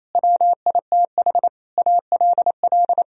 Morse code, heard as a keyed tone, about 28 wpm, 700 Hz. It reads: WITH ALL
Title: Sentences from Top 1000 Words - Review Part 2 - 28wpm